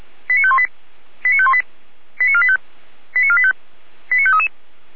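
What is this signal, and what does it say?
Five-tone selective call (ZVEI), heard as quick runs of five tones, tones of about 70 ms. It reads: 89518 89518 89485 89485 89533
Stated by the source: ZVEI five-tone call received by SDR, recorded by the Orlamünde fire brigade (feuerwehr-orlamuende).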